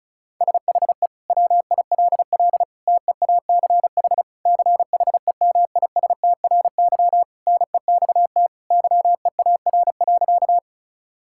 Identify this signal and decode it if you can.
Morse code, heard as a keyed tone, about 36 wpm, 700 Hz. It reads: SHE WILL TEACH CHEMISTRY NEXT YEAR.
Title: Sentences from Top 1000 Words - Review Part 6 - 35wpm